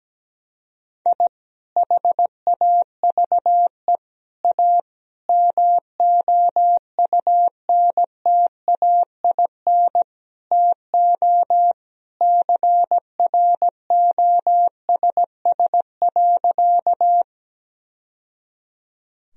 Morse code, heard as a keyed tone, about 17 wpm, 700 Hz. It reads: I HAVE A MOUNTAIN TO CROSS.
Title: Sentences from Top 1000 Words - Review Part 1 - 17wpm